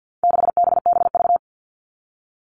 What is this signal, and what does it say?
Morse code, heard as a keyed tone, about 50 wpm, 700 Hz. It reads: 6BBV
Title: All Cut Numbers - 4-Digit 50wpm